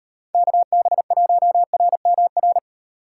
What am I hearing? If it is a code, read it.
Morse code, heard as a keyed tone, about 38 wpm, 700 Hz. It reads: KB1RMR